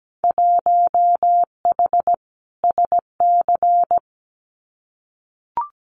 Morse code, heard as a keyed tone, about 17 wpm, 700 Hz. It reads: 1H SC
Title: Winter Field Day Exchanges 17wpm